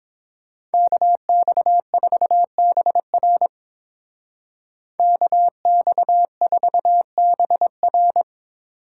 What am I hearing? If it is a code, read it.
Morse code, heard as a keyed tone, about 27 wpm, 700 Hz. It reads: KX4BR KX4BR
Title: Speed Racing - USA Call Signs 22wpm